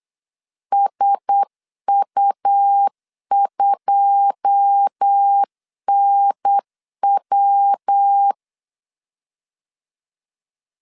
Morse code, heard as a keyed tone, about 8 wpm, 800 Hz. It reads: SU2NW
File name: LEZIONI / 9 2 8wpm 3437